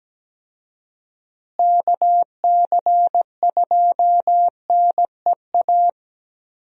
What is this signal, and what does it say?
Morse code, heard as a keyed tone, about 17 wpm, 700 Hz. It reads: KC2NEA
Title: USA Call Signs - 6 Characters - Repeated 2x - 17wpm